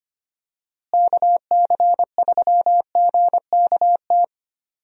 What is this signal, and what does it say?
Morse code, heard as a keyed tone, about 25 wpm, 700 Hz. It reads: KC3GKT
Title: USA Call Signs - 6 Characters - Repeated 3x - 25wpm